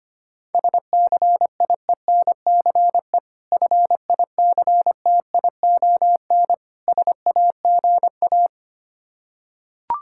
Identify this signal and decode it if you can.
Morse code, heard as a keyed tone, about 25 wpm, 690 Hz. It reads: SCIENCE FICTION SAGA
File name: practice / 25wpm / Difficult Words with Hints - 25wpm